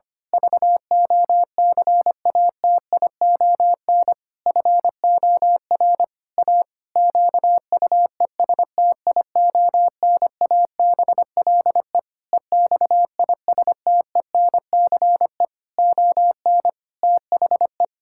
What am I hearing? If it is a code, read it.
Morse code, heard as a keyed tone, about 25 wpm, 700 Hz. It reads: VOCATION FOR A QUESTIONABLE EXISTENCE ON THE
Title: Pangram Challenge v2 - 25wpm